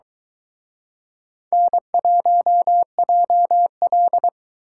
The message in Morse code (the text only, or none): N1JL